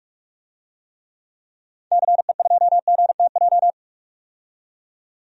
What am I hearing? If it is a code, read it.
Morse code, heard as a keyed tone, about 45 wpm, 700 Hz. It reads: CE2GTJ
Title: DXCC-100 Call Signs - 6 Characters - 45wpm